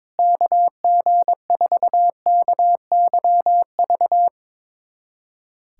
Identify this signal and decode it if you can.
Morse code, heard as a keyed tone, about 21 wpm, 700 Hz. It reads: KG4KYV
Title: USA Callsign DE Callsign - 6 Characters - 22wpm